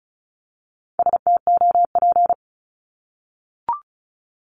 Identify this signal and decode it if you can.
Morse code, heard as a keyed tone, about 34 wpm, 700 Hz. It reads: STOP
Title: Top 500 Words 35wpm